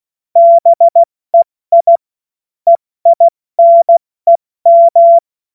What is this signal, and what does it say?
Morse code, heard as a keyed tone, about 16 wpm, 680 Hz. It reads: BEI EINEM
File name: Morsekurs / Witze / Witz 07.November 2023 70 F80